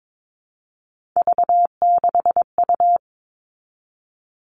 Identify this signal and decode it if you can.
Morse code, heard as a keyed tone, about 22 wpm, 700 Hz. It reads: V6U
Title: Three Challenging Letter-Number Combinations 22wpm